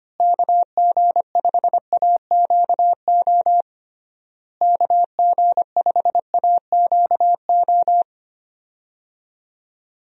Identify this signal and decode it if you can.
Morse code, heard as a keyed tone, about 25 wpm, 700 Hz. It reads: KG5AQO KG5AQO
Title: USA Call Signs - 6 Characters - Repeated 3x - 25wpm